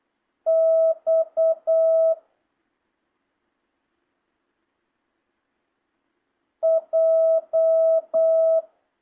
Morse code, heard as a keyed tone, about 8 wpm, 650 Hz. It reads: X J